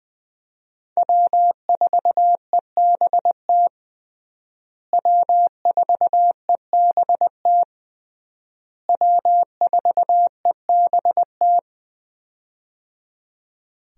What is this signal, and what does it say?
Morse code, heard as a keyed tone, about 20 wpm, 700 Hz. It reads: W4EBT W4EBT W4EBT
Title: USA Call Signs - 5 Characters - Repeated 3x - 20wpm